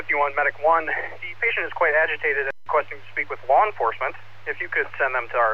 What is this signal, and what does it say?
Emergency Alert System (SAME) tones, heard as harsh buzzing bursts, with speech. It is heard from 1.1 to 5.1 s.